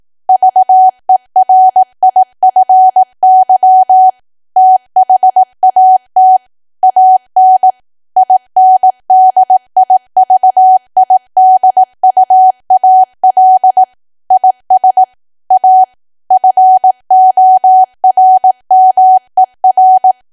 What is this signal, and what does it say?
Morse code, heard as a keyed tone, about 18 wpm, 750 Hz. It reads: VERIFY THAT AN INDIVIDUAL IS A FORMER
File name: cwa / W1AW-Level2 / 18WPM